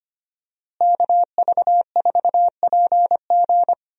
Morse code, heard as a keyed tone, about 25 wpm, 700 Hz. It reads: KV4PG
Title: USA Call Signs - 5 Characters - Repeated 3x - 25wpm